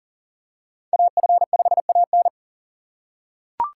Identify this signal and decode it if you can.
Morse code, heard as a keyed tone, about 40 wpm, 700 Hz. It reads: AF5AN
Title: USA Call Signs - 5 Characters - 4x Character Spacing - 40wpm